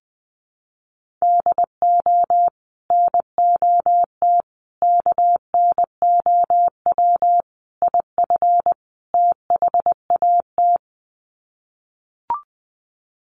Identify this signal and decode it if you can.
Morse code, heard as a keyed tone, about 20 wpm, 700 Hz. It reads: DO NOT KNOW IF THAT
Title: Sets of 5 Words Encore2 20wpm